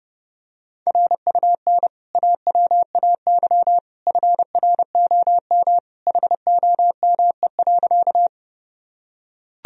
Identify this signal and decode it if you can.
Morse code, heard as a keyed tone, about 30 wpm, 700 Hz. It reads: RUN AWAY FROM HOME.
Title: Sentences from Top 1000 Words - Review Part 1 - 30wpm